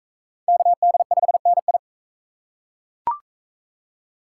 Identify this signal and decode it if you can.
Morse code, heard as a keyed tone, about 41 wpm, 700 Hz. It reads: KD5NI